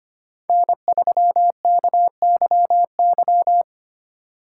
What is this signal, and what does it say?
Morse code, heard as a keyed tone, about 25 wpm, 700 Hz. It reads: N3KYY
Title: USA Call Signs - 4x Character Spacing - 25wpm